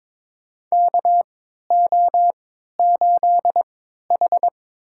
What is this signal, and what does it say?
Morse code, heard as a keyed tone, about 23 wpm, spaced as for 13 wpm, 700 Hz. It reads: KO8H
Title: USA Call Signs - 4 Characters - 3x Character Spacing - 22wpm